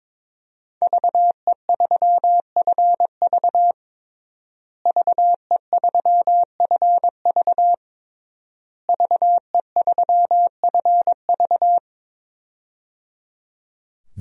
Morse code, heard as a keyed tone, about 22 wpm, 700 Hz. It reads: VE3FV VE3FV VE3FV
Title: DXCC-100 Call Signs - 5 Characters - Repeated 3x - 22wpm